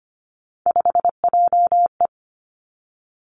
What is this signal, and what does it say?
Morse code, heard as a keyed tone, about 25 wpm, 700 Hz. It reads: 5JE
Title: Three Challenging Letter-Number Combinations 25wpm